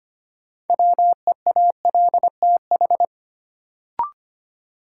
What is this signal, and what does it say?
Morse code, heard as a keyed tone, about 25 wpm, 700 Hz. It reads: WEALTH
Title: Sentences from Top 1200 Words - New Word Drill Alt - 25wpm